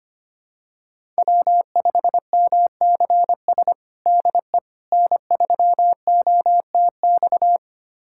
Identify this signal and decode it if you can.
Morse code, heard as a keyed tone, about 25 wpm, 700 Hz. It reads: W5MCS DE N3OTX
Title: USA Callsign DE Callsign - 5 Characters - 25wpm